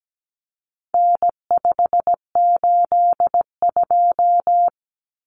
Morse code, heard as a keyed tone, about 17 wpm, 700 Hz. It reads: N582
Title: Common Cut Numbers - 4-Digit 17wpm